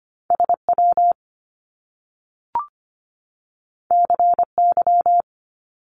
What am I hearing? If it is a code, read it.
Morse code, heard as a keyed tone, about 25 wpm, 700 Hz. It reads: SW CY
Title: Top Two Letter Combinations in Words 25wpm